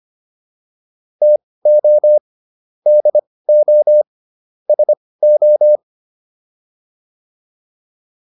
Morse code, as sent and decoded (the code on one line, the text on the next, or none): - --- / -.. --- / ... ---
TO DO SO